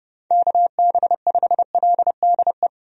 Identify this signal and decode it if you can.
Morse code, heard as a keyed tone, about 30 wpm, 700 Hz. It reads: KB5LDE
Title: USA Call Signs - 6 Characters - 30wpm